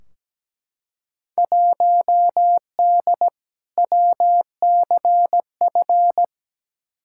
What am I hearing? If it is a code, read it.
Morse code, heard as a keyed tone, about 17 wpm, 700 Hz. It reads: 1D WCF
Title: Field Day Exchanges - 17wpm